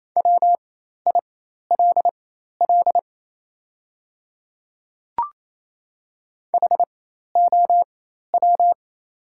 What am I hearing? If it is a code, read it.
Morse code, heard as a keyed tone, about 28 wpm, spaced as for 13 wpm, 700 Hz. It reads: WILL HOW
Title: 4x Character Spacing - Top 100 Words - 28wpm